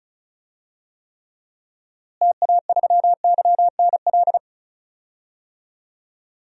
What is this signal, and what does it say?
Morse code, heard as a keyed tone, about 34 wpm, 700 Hz. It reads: TA3YNL
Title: DXCC-100 Call Signs - 6 Characters - Repeated 2x - 35wpm